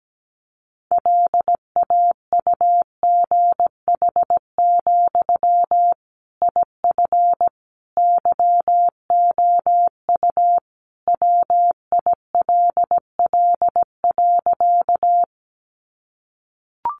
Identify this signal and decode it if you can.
Morse code, heard as a keyed tone, about 17 wpm, 700 Hz. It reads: LAUGH, IF YOU WILL.
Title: Sentences from Top 500 Words 17wpm